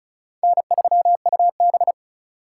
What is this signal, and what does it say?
Morse code, heard as a keyed tone, about 36 wpm, 700 Hz. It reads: N3UB
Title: USA Call Signs - 4 Characters - 4x Character Spacing - 35wpm